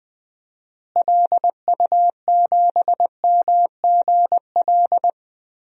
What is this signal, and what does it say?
Morse code, heard as a keyed tone, about 21 wpm, 700 Hz. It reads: LU7MGL
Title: DXCC-100 Call Signs - 6 Characters - Repeated 2x - 20wpm